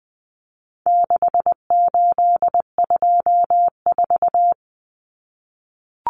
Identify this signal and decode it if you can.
Morse code, heard as a keyed tone, about 20 wpm, 700 Hz. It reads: 6824
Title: Common Cut Numbers - 4-Digit 20wpm